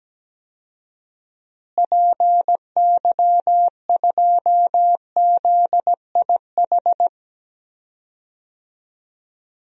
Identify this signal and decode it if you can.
Morse code, heard as a keyed tone, about 17 wpm, 700 Hz. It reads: PY2ZIH
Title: DXCC-100 Call Signs - 6 Characters - Repeated 3x - 17wpm